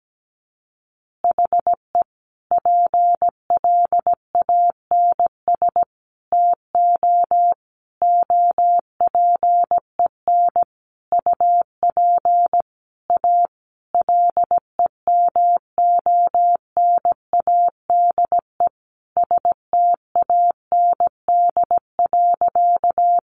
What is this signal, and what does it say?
Morse code, heard as a keyed tone, about 17 wpm, 700 Hz. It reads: HE PLANS TO OPEN UP A LEMONADE STAND.
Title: Additional Easy Phrases 17wpm